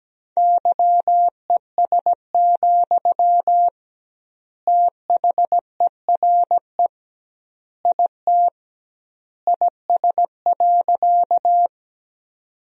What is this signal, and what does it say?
Morse code, heard as a keyed tone, about 17 wpm, 700 Hz. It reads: YES, THERE IT IS.
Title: 2x Word-Spacing - Sentences from Top 500 Words - 17wpm